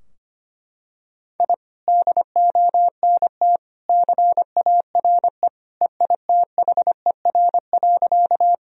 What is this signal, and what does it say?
Morse code, heard as a keyed tone, about 25 wpm, 700 Hz. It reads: I DONT CARE EITHER.
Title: Sentences from Top 1000 Words - Review Part 4 - 25wpm